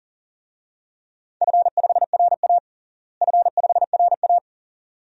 Morse code, heard as a keyed tone, about 40 wpm, 700 Hz. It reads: F5RA F5RA
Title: DXCC-100 Call Signs - 4 Characters - Repeated 2x - 40wpm